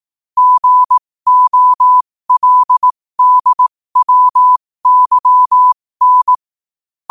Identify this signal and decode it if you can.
Morse code, heard as a keyed tone, about 18 wpm, spaced as for 15 wpm, 1000 Hz. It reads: GOLDWYN
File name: qotdfiles / 15wpm